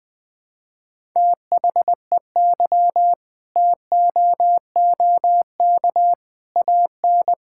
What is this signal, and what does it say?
Morse code, heard as a keyed tone, about 20 wpm, 700 Hz. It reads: THEY TOOK AN
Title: Sentences from Top 1000 Words - Review Part 6 - 20wpm